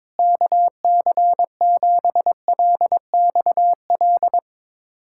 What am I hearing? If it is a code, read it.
Morse code, heard as a keyed tone, about 22 wpm, 700 Hz. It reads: KC7LXL
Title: USA Callsign DE Callsign - 6 Characters - 22wpm